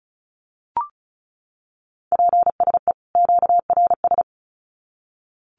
Morse code, heard as a keyed tone, about 34 wpm, 700 Hz. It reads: PSE QRS
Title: Common QSO Phrases 35wpm